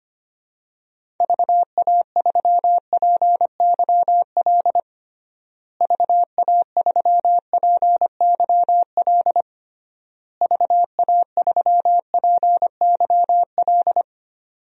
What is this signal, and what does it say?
Morse code, heard as a keyed tone, about 25 wpm, 700 Hz. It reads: VA3PYL VA3PYL VA3PYL
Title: DXCC-100 Call Signs - 6 Characters - Repeated 3x - 25wpm